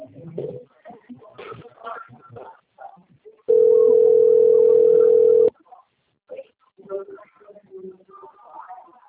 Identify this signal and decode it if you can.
Ringback tone (North America), 440 + 480 Hz.
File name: rg-1600-unknown- - - .9149